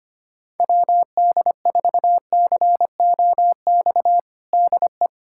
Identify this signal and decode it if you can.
Morse code, heard as a keyed tone, about 25 wpm, 700 Hz. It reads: WD4COX DE
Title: USA Callsign DE Callsign 25wpm